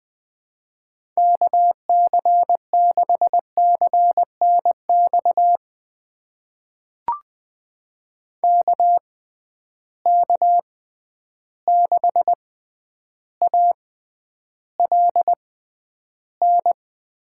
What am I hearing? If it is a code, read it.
Morse code, heard as a keyed tone, about 20 wpm, spaced as for 10 wpm, 700 Hz. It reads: KC6CNX KK6ALN
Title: USA Call Signs - 6 Characters - 6x Character Spacing - 20wpm